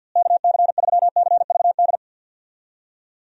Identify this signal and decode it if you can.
Morse code, heard as a keyed tone, about 51 wpm, 700 Hz. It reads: KC3CVD
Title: USA Call Signs - 6 Characters - Repeated 3x - 50wpm